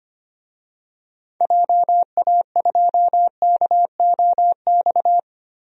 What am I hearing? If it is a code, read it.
Morse code, heard as a keyed tone, about 25 wpm, 700 Hz. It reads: JA2KOX
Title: DXCC-100 Call Signs - 6 Characters - 25wpm